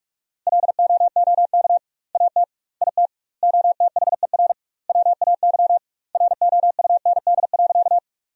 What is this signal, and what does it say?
Morse code, heard as a keyed tone, about 46 wpm, 700 Hz. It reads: LOOK AT IT OTHER WAY ROUND.
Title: 2x Word-Spacing - Sentences from Top 200 Words - 45wpm